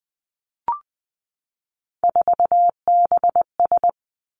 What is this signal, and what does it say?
Morse code, heard as a keyed tone, about 20 wpm, 700 Hz. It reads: 4BS
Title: Three Challenging Letter-Number Combinations 20wpm